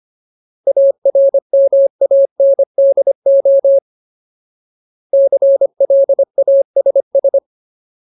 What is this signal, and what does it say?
Morse code, heard as a keyed tone, about 25 wpm, 550 Hz. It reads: ARMANDO CLASS
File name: Ultimate Word List / 25WPM Ultimate